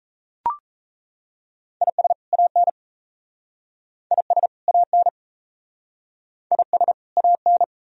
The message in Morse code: IS AN IS AN IS AN

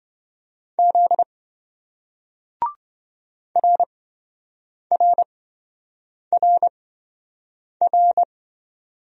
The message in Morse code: Z RRRR

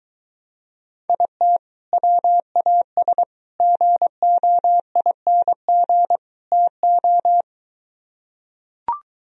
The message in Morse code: IT WAS GOING TO

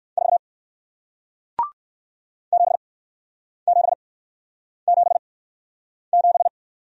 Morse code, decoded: 4 7777

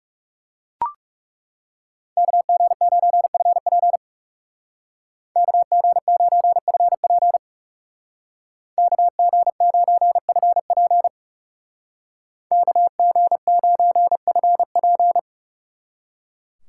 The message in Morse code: KG9FP KG9FP KG9FP KG9FP

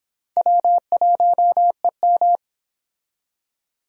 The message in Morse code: W1EM

